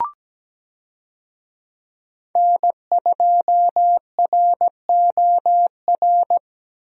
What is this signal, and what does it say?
Morse code, heard as a keyed tone, about 17 wpm, 700 Hz. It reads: N2ROR
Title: USA Call Signs - 5 Characters - Repeated 2x - 17wpm